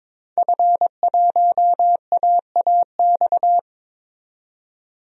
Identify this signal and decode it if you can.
Morse code, heard as a keyed tone, about 22 wpm, 700 Hz. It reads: F1AAX
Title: DXCC-100 Call Signs - 5 Characters - 22wpm